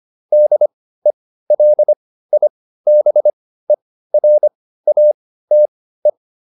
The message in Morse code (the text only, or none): DELIBERATE